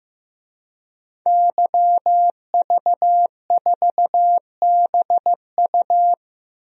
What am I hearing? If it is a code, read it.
Morse code, heard as a keyed tone, about 15 wpm, 700 Hz. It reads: YV4BU
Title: DXCC-100 Call Signs - 5 Characters - Repeated 3x - 15wpm